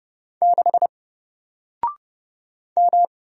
Morse code, heard as a keyed tone, about 30 wpm, 700 Hz. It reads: 6 M
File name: Single-Letter-Number-30wpm